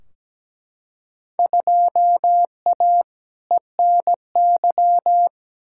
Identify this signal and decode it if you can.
Morse code, heard as a keyed tone, about 17 wpm, 700 Hz. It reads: 2A ENY
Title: Field Day Exchanges - 17wpm